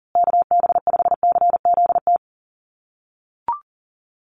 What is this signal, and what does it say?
Morse code, heard as a keyed tone, about 40 wpm, 700 Hz. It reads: KB5CZT
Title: USA Call Signs Repeated 2x 40wpm